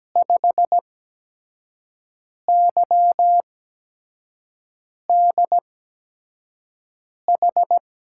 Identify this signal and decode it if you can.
Morse code, heard as a keyed tone, about 17 wpm, spaced as for 5 wpm, 700 Hz. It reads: 5YDH
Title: USA Call Signs - 8x Character Spacing - 17wpm